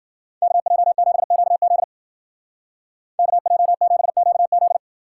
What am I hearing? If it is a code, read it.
Morse code, heard as a keyed tone, about 52 wpm, 700 Hz. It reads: KJ7QZ KJ7QZ